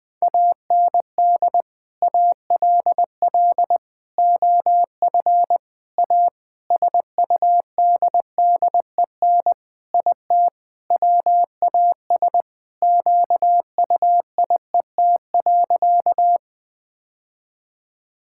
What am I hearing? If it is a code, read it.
Morse code, heard as a keyed tone, about 20 wpm, 700 Hz. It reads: AND ALL OF A SUDDEN IT WAS QUIET.
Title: Sentences with 5-Word Phrases - Spoken Beginning - 20wpm